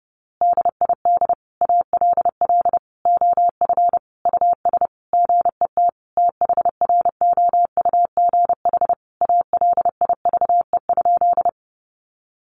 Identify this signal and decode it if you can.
Morse code, heard as a keyed tone, about 30 wpm, 700 Hz. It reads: DID ALL OF US GET THROUGH ALIVE?
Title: A Sound Of Thunder 30wpm